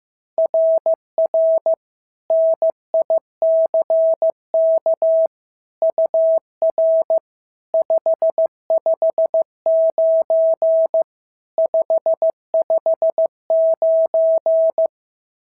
Morse code, heard as a keyed tone, about 15 wpm, 690 Hz. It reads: RR NICK UR 559 559
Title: SOTA QSO Practice 15wpm